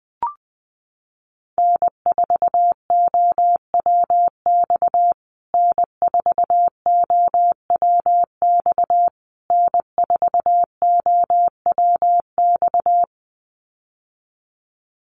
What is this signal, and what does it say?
Morse code, heard as a keyed tone, about 19 wpm, 700 Hz. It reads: N4OWX N4OWX N4OWX
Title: USA Call Signs Repeated 3x 20wpm